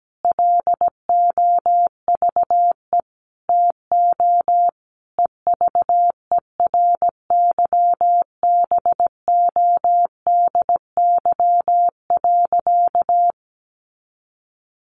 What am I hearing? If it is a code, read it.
Morse code, heard as a keyed tone, about 17 wpm, 700 Hz. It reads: LOVE TO EVERYBODY.
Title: Sentences from Top 1400 Words 17wpm